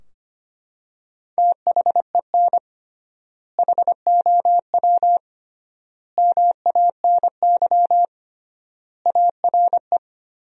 Morse code, heard as a keyed tone, about 25 wpm, 700 Hz. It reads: THEN HOW MANY ARE
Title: 3x Word-Spacing - Sentences from Top 100 Words - 25wpm